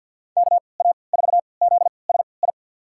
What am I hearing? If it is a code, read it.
Morse code, heard as a keyed tone, about 46 wpm, spaced as for 27 wpm, 700 Hz. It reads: KA4ZSI
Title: USA Call Signs - 6 Characters - 3x Character Spacing - 50wpm